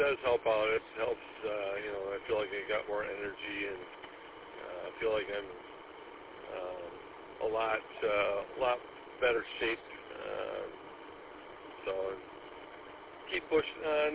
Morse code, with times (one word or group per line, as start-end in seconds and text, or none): none